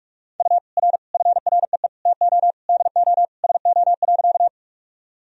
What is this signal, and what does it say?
Morse code, heard as a keyed tone, about 41 wpm, 690 Hz. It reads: U R FREE TO DO SO.
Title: Sentences from Top 500 Words and Abbreviations 45wpm